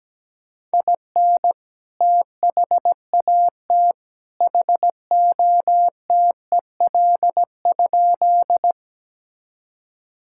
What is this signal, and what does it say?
Morse code, heard as a keyed tone, about 17 wpm, 700 Hz. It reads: IN THAT HOTEL?
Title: Sentences from Top 1600 Words 17wpm